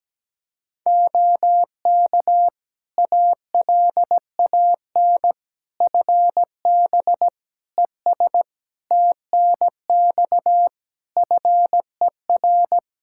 Morse code, heard as a keyed tone, about 17 wpm, 700 Hz. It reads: OK ALAN FB ES TNX FER
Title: Practice QSO - Exchange 2 - Info - 17wpm